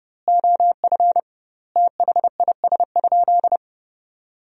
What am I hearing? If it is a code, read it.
Morse code, heard as a keyed tone, about 30 wpm, 700 Hz. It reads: OF THIS?
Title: 2x Word-Spacing - Sentences from Top 200 Words - 30wpm